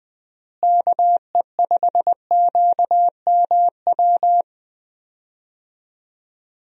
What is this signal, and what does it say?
Morse code, heard as a keyed tone, about 20 wpm, 700 Hz. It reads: KE5QMW
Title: USA Call Signs - 6 Characters - Repeated 3x - 20wpm